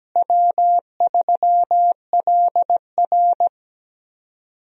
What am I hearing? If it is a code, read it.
Morse code, heard as a keyed tone, about 17 wpm, 700 Hz. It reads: W3LR
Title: USA Call Signs - 4 Characters - Repeated 3x - 17wpm